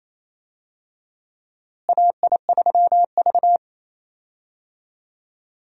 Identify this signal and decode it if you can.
Morse code, heard as a keyed tone, about 28 wpm, 700 Hz. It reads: AI3V
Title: USA Call Signs - 4 Characters - Repeated 2x - 28wpm